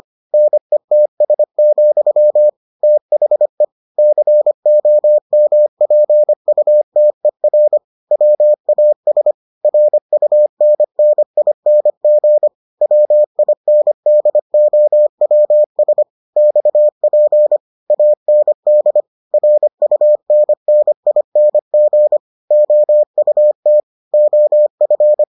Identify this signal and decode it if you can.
Morse code, heard as a keyed tone, about 25 wpm, 600 Hz. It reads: NETS, THE COMPUTER WAS RUNNING WINDOWS XP AND RUNNING OUT OF